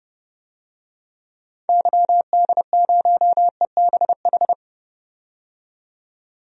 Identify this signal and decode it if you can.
Morse code, heard as a keyed tone, about 30 wpm, 700 Hz. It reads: YD0EBH
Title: DXCC-100 Call Signs - 6 Characters - 30wpm